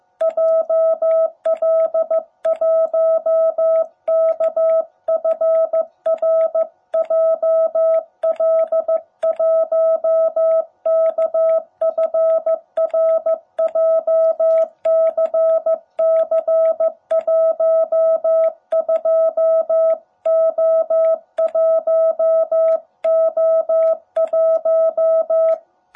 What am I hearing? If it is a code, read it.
Morse code, heard as a keyed tone, about 15 wpm, 660 Hz. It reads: JL1KFR JL1KFR JCC12O1O1
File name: jh8cbh / 2021cw